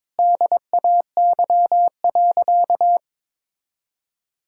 Morse code, heard as a keyed tone, about 22 wpm, 700 Hz. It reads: DAY.
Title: Sentences from Top 1000 Words - Review Part 5 - 22wpm